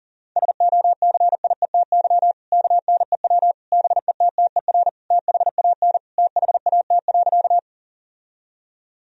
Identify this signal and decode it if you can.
Morse code, heard as a keyed tone, about 40 wpm, 700 Hz. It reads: SOCIETY KNEW BETTER THAN THAT.
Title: Sentences from Top 1000 Words - Review Part 4 - 40wpm